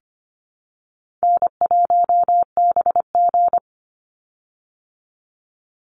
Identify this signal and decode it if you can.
Morse code, heard as a keyed tone, about 25 wpm, 700 Hz. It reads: N1BG
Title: Call Signs Repeated 2x - 25wpm